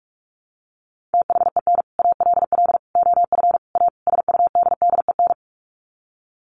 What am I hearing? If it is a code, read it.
Morse code, heard as a keyed tone, about 44 wpm, 700 Hz. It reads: THEN ALL OF A SUDDEN
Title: Sets of 5 Words Encore 45wpm